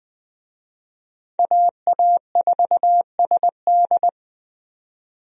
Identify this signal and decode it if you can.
Morse code, heard as a keyed tone, about 21 wpm, 700 Hz. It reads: AA4SD
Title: USA Call Signs - 5 Characters - 20wpm